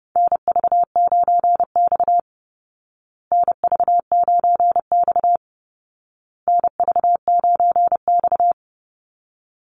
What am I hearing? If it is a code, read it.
Morse code, heard as a keyed tone, about 30 wpm, 700 Hz. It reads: NV9X NV9X NV9X
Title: Call Signs Repeated 2x - 30wpm